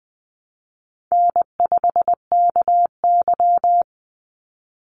Morse code, heard as a keyed tone, about 19 wpm, 700 Hz. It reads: N5KY
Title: Easy Call Signs - 20wpm